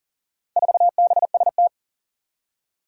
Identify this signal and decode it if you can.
Morse code, heard as a keyed tone, about 40 wpm, 700 Hz. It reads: 4BST
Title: Four Challenging Letter-Number Combinations 40wpm